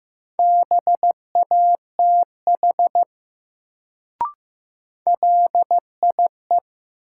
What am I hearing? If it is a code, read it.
Morse code, heard as a keyed tone, about 15 wpm, 700 Hz. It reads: BATH LIE
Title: Sentences from Top 1600 Words - New Word Drill - 15wpm